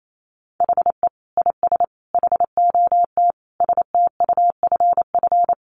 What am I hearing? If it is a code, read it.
Morse code, heard as a keyed tone, about 28 wpm, 700 Hz. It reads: HE IS HOT STUFF
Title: Sentences from Top 1500 Words 28wpm